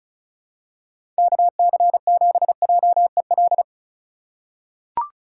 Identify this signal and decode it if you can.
Morse code, heard as a keyed tone, about 34 wpm, 700 Hz. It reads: KC7JEL
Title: USA Call Signs - 6 Characters - 4x Character Spacing - 35wpm